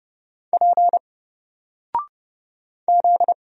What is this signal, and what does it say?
Morse code, heard as a keyed tone, about 30 wpm, 700 Hz. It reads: P Z
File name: Single-Letter-Number-30wpm